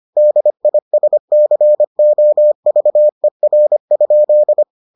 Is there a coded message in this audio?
Yes: Morse code from 0.2 to 4.6 s, heard as a keyed tone, about 25 wpm, 600 Hz. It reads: DISCOVER?